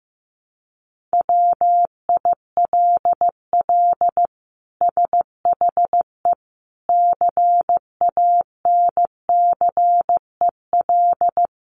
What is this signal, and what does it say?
Morse code, heard as a keyed tone, about 15 wpm, 700 Hz. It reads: WILL SHE CANCEL
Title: Sentences from Top 1200 Words 15wpm